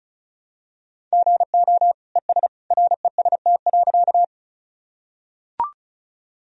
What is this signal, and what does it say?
Morse code, heard as a keyed tone, about 34 wpm, 700 Hz. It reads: GO ES REST.
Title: Sentences from Top 500 Words and Abbreviations 35wpm